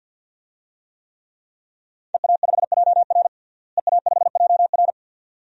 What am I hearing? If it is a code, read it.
Morse code, heard as a keyed tone, about 50 wpm, 700 Hz. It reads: EA5JR EA5JR
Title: DXCC-100 Call Signs - 5 Characters - Repeated 2x - 50wpm